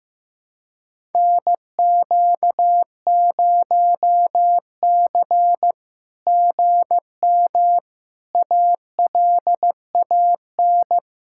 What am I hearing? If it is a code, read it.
Morse code, heard as a keyed tone, about 14 wpm, 700 Hz. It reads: NQ0C GM ALAN
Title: SOTA QSO Practice 15wpm